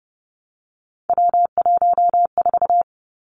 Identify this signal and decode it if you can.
Morse code, heard as a keyed tone, about 31 wpm, 700 Hz. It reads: W14
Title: Three Challenging Letter-Number Combinations 30wpm